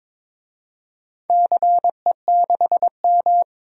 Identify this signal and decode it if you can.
Morse code, heard as a keyed tone, about 22 wpm, 700 Hz. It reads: CE6M
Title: DXCC-100 Call Signs - 4 Characters - 22wpm